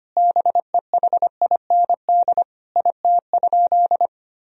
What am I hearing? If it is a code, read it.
Morse code, heard as a keyed tone, about 25 wpm, 700 Hz. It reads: BEHIND IT?
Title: Sentences from Top 1000 Words - Review Part 4 - 25wpm